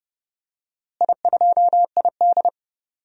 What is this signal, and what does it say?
Morse code, heard as a keyed tone, about 30 wpm, 700 Hz. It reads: I2ID
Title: DXCC-100 Call Signs - 4 Characters - Repeated 3x - 30wpm